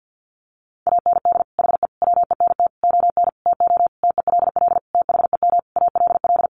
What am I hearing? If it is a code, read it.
Morse code, heard as a keyed tone, about 49 wpm, 700 Hz. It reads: AND HE WENT ON TO TELL THEM ALL
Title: Sentences from Top 900 Words 50wpm